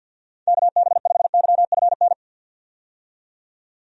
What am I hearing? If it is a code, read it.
Morse code, heard as a keyed tone, about 51 wpm, 700 Hz. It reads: KB5YFN
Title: USA Call Signs - 6 Characters - 50wpm